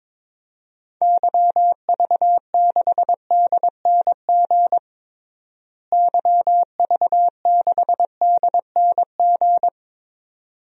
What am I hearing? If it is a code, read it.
Morse code, heard as a keyed tone, about 22 wpm, 700 Hz. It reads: YV6DNG YV6DNG
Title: DXCC-100 Call Signs - 6 Characters - Repeated 2x - 22wpm